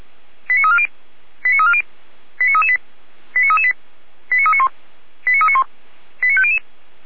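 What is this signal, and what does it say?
Five-tone selective call (ZVEI), heard as quick runs of five tones, tones of about 80 ms. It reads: 89340 89340 89338 89338 89391 89391 89500
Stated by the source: ZVEI five-tone call 89304 received on SDR.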